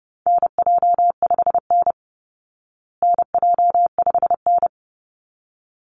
Morse code, heard as a keyed tone, about 30 wpm, 700 Hz. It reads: NJ5N NJ5N
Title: Call Signs Repeated 3x - 30wpm